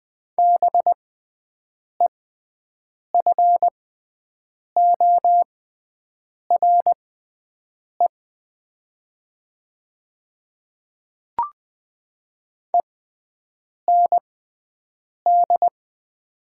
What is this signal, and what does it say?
Morse code, heard as a keyed tone, about 20 wpm, spaced as for 7 wpm, 700 Hz. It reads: BEFORE END
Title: 6x Character Spacing - Top 200 Words - 20wpm